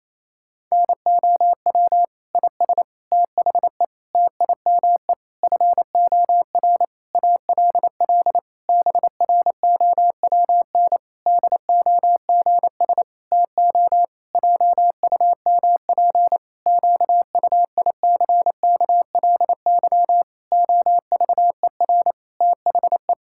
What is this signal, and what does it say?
Morse code, heard as a keyed tone, about 28 wpm, 700 Hz. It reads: NOW IS THE TIME FOR ALL BROWN DOGS TO JUMP QUICKLY OVER THE
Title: Pangram Challenge v2 - 28wpm